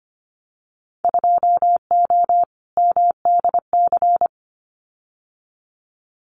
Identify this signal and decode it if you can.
Morse code, heard as a keyed tone, about 25 wpm, 700 Hz. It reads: 2O MDC
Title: Winter Field Day Exchanges 25wpm